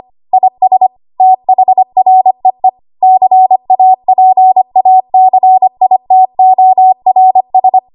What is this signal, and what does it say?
Morse code, heard as a keyed tone, about 25 wpm, 750 Hz. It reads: IS THREE CAPACITORS
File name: Archive / 25 WPM / 25WPM